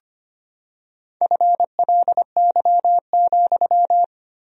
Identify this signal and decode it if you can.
Morse code, heard as a keyed tone, about 25 wpm, 700 Hz. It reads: FLY,
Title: 4x Word-Spacing - Sentences from Top 500 Words - 25wpm